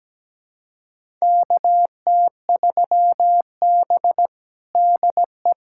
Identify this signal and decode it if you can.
Morse code, heard as a keyed tone, about 17 wpm, 700 Hz. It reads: KT3B DE
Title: USA Callsign DE Callsign - 4 Characters - 17wpm